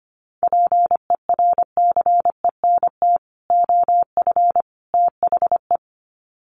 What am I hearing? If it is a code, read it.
Morse code, heard as a keyed tone, about 25 wpm, 700 Hz. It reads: PERCENT OF THE
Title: Sets of 3 Words 25wpm